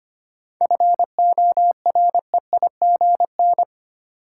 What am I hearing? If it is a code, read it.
Morse code, heard as a keyed tone, about 25 wpm, 700 Hz. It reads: FOREIGN
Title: Seven Letter Words - 25wpm